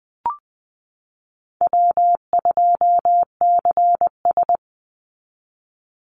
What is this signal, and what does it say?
Morse code, heard as a keyed tone, about 20 wpm, 700 Hz. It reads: W2CS
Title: Easy Call Signs - 20wpm